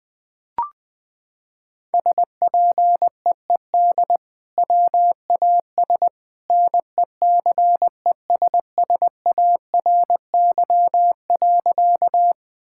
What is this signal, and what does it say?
Morse code, heard as a keyed tone, about 20 wpm, 700 Hz. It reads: SPEED WAS NECESSARY.
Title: Sentences from Top 1000 Words - Review Part 5 - 20wpm